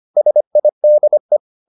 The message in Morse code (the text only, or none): SIDE